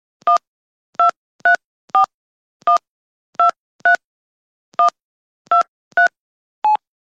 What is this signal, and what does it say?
DTMF, heard as telephone touch tones, tones of about 120 ms. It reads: 1234123123
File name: sound-effect / dialing / fara-download Phone Dialing sfx 2